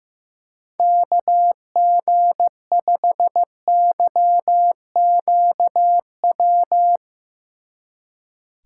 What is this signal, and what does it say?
Morse code, heard as a keyed tone, about 15 wpm, 700 Hz. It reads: KG5YQW
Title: Speed Racing - USA 2x3 Call Signs 15wpm